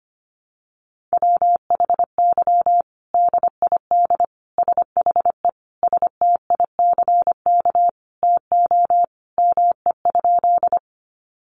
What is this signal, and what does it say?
Morse code, heard as a keyed tone, about 25 wpm, 700 Hz. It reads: WHY DID SHE STICK TO ME?
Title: Sentences from Top 1200 Words 25wpm